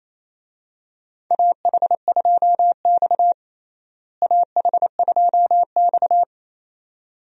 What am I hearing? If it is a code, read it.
Morse code, heard as a keyed tone, about 28 wpm, 700 Hz. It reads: AH2X AH2X
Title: USA Call Signs - 4 Characters - Repeated 2x - 28wpm